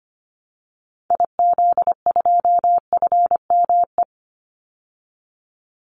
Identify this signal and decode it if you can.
Morse code, heard as a keyed tone, about 25 wpm, 700 Hz. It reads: IZ2FME
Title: Call Signs Repeated 2x - 25wpm